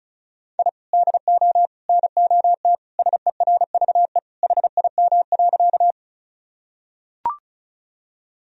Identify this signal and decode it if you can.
Morse code, heard as a keyed tone, about 36 wpm, 700 Hz. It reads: I DO NOT SERVE HIM.
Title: Sentences from Top 1000 Words - Review Part 2 - 35wpm